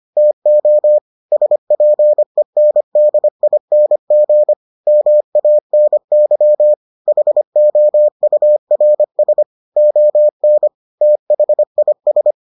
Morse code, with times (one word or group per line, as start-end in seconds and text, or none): 0.2-1.0 TO
1.3-4.5 SPENDING
4.9-6.7 MANY
7.1-9.4 HOURS
9.8-10.7 ON
11.0-12.3 THIS